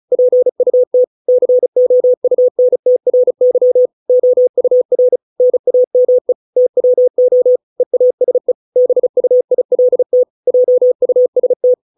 Morse code, heard as a keyed tone, about 34 wpm, 500 Hz. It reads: PUT COUNTRY OUR NAME TWO EASE BUILT JUST